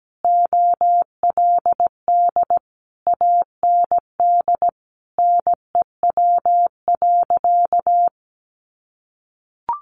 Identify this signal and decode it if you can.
Morse code, heard as a keyed tone, about 17 wpm, 700 Hz. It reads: OLD AND NEW.
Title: Sentences from Top 1500 Words 17wpm